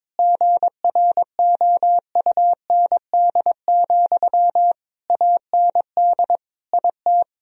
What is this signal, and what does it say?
Morse code, heard as a keyed tone, about 21 wpm, 700 Hz. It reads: GROUND, AND IT
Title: Sentences from Top 1000 Words - Review Part 2 - 22wpm